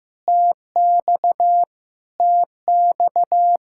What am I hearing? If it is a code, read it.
Morse code, heard as a keyed tone, about 15 wpm, 700 Hz. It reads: TX TX
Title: Practice QSO - Exchange 1 - Beginning - 15wpm